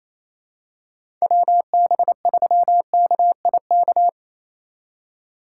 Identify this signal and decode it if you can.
Morse code, heard as a keyed tone, about 28 wpm, 700 Hz. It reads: WB3KIK
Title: USA Call Signs - 6 Characters - 2x Character Spacing - 28wpm